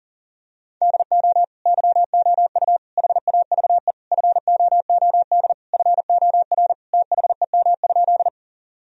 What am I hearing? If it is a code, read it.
Morse code, heard as a keyed tone, about 40 wpm, 700 Hz. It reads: DO YOU HAVE FOOD FOR THEM?
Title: Sentences from Top 1000 Words - Review Part 1 - 40wpm